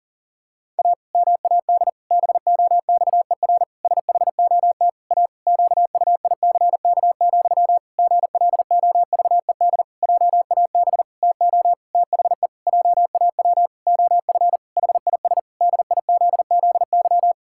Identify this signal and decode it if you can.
Morse code, heard as a keyed tone, about 40 wpm, 700 Hz. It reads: A MAD BOXER SHOT A QUICK, GLOVED JAB TO THE JAW OF HIS DIZZY
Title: Pangram Challenge v2 - 40wpm